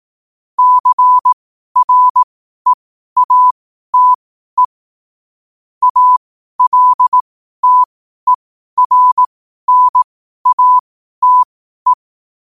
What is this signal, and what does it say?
Morse code, heard as a keyed tone, about 18 wpm, spaced as for 12 wpm, 1000 Hz. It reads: CREATE ALTERNATE
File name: qotdfiles / 12wpm